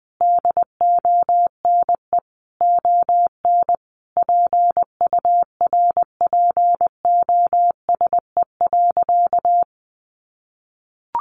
Morse code, heard as a keyed tone, about 19 wpm, 700 Hz. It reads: DONE ON PURPOSE.
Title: Sentences from Top 800 Words 20wpm